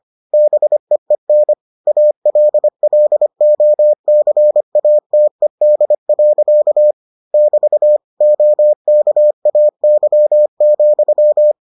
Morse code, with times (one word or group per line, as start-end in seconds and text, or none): 0.3-1.5 BEEN
1.9-6.9 ALLOCATED.
7.3-8.0 <BT>
8.2-11.5 OKAY,